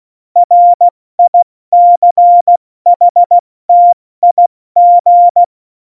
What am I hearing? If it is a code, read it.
Morse code, heard as a keyed tone, about 16 wpm, 700 Hz. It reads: RICHTIG
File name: Morsekurs / Witze / Witz 13. Oktober 2020, 70 F80